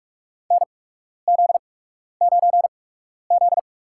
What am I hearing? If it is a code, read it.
Morse code, heard as a keyed tone, about 43 wpm, spaced as for 12 wpm, 700 Hz. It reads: NZ9Z
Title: USA Call Signs - 4 Characters - 8x Character Spacing - 45wpm